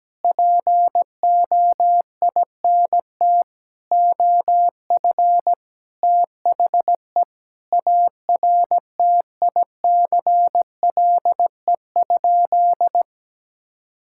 Morse code, heard as a keyed tone, about 17 wpm, 700 Hz. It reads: POINT OF THE ARTICLE?
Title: Sentences from Top 1000 Words - Review Part 5 - 17wpm